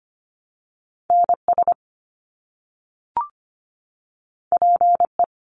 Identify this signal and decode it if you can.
Morse code, heard as a keyed tone, about 25 wpm, 700 Hz. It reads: NS PE
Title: Canadian Provinces and Territories - Repeated 3x - 25wpm